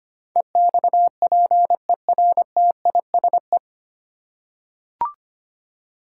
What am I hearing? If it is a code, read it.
Morse code, heard as a keyed tone, about 25 wpm, 700 Hz. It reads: EXPERTISE
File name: Nine Letter Words - 25wpm